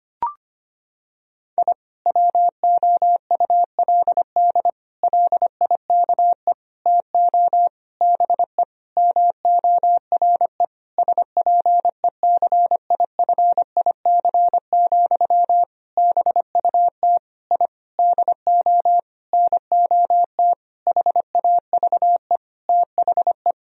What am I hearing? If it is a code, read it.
Morse code, heard as a keyed tone, about 25 wpm, 700 Hz. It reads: I WOULD LIKE TO BE MORE SPECIFIC, BUT I DO NOT HAVE THE
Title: Sentences from Top 1000 Words - Review Part 3 - 25wpm